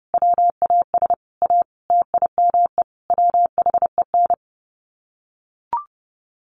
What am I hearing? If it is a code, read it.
Morse code, heard as a keyed tone, about 30 wpm, 700 Hz. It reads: WAS A TIME WHEN
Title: Sets of 5 Words 30wpm